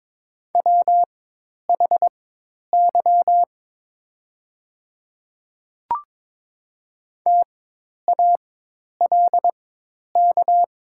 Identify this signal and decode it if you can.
Morse code, heard as a keyed tone, about 23 wpm, spaced as for 10 wpm, 700 Hz. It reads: WHY TALK
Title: 4x Character Spacing - Top 500 Words - 22wpm